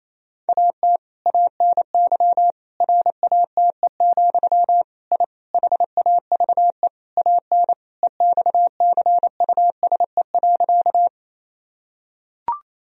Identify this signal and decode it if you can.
Morse code, heard as a keyed tone, about 28 wpm, 700 Hz. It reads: AT ANY RATE, I HAVE AN EXCUSE.
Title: Sentences from Top 1000 Words - Review Part 4 - 28wpm